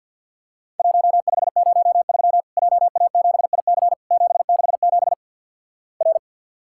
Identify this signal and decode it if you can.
Morse code, heard as a keyed tone, about 50 wpm, 700 Hz. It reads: 1503 JA7IC 767 R
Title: Worked All Europe DX Contest - QTC Exchange - 50wpm